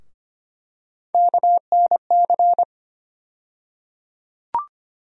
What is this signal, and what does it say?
Morse code, heard as a keyed tone, about 25 wpm, 700 Hz. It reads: KNC
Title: USA Call Sign Postfixes - 3 Letters 25wpm